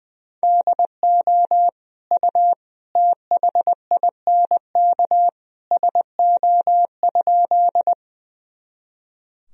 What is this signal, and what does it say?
Morse code, heard as a keyed tone, about 21 wpm, 700 Hz. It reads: DO U THINK SO?
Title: Sentences from Top 200 Words and Abbreviations 20wpm